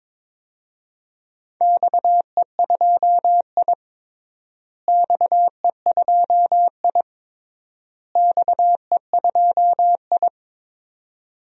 Morse code, heard as a keyed tone, about 22 wpm, 700 Hz. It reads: XE2I XE2I XE2I
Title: DXCC-100 Call Signs - 4 Characters - Repeated 3x - 22wpm